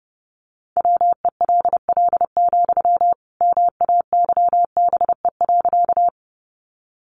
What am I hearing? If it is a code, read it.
Morse code, heard as a keyed tone, about 30 wpm, 700 Hz. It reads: WELL, MAYBE.
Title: Sentences from Top 1000 Words 30wpm